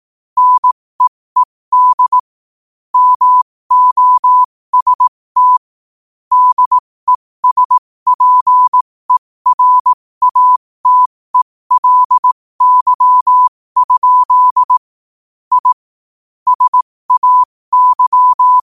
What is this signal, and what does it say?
Morse code, heard as a keyed tone, about 18 wpm, spaced as for 15 wpm, 1000 Hz. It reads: NEED MOST DESPERATELY? I SAY
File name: qotdfiles / 15wpm